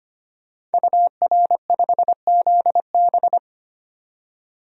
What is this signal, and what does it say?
Morse code, heard as a keyed tone, about 25 wpm, 700 Hz. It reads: UR5ZB
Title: DXCC-100 Call Signs - 5 Characters - 25wpm